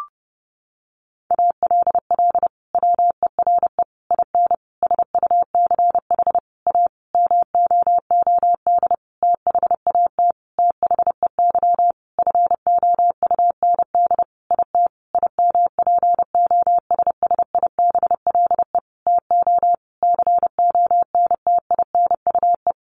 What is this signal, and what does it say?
Morse code, heard as a keyed tone, about 30 wpm, 700 Hz. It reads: ALL WERE IN SUCH A MOOD THAT THEY FOUND IT IMPOSSIBLE TO CONTINUE
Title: Sentences from Top 1100 Words 30wpm